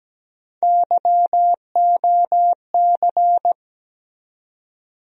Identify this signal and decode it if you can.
Morse code, heard as a keyed tone, about 17 wpm, 700 Hz. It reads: YOC
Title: USA Call Sign Postfixes - 3 Letters 17wpm